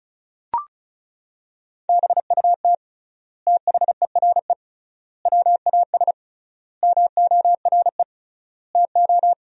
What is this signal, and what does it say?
Morse code, heard as a keyed tone, about 36 wpm, 700 Hz. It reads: BUT THERE WAS MORE TO
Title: 3x Word-Spacing - Sentences from Top 100 Words - 35wpm